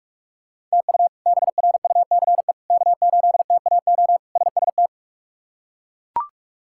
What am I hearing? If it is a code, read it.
Morse code, heard as a keyed tone, about 42 wpm, 700 Hz. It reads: TU BRUCE K8TAO SST
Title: K1USN SST QSO Practice - 45wpm